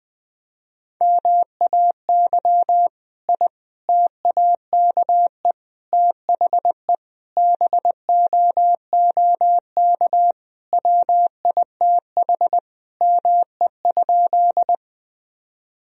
Morse code, heard as a keyed tone, about 20 wpm, 700 Hz. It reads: MAY I TAKE THE BOOK WITH ME?
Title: Sentences from Top 1000 Words - Review Part 1 - 20wpm